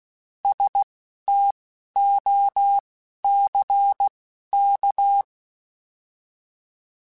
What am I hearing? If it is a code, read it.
Morse code, heard as a keyed tone, about 16 wpm, spaced as for 11 wpm, 800 Hz. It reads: STOCK
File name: Alphabetical / Alphabetical mp3s / Ss @10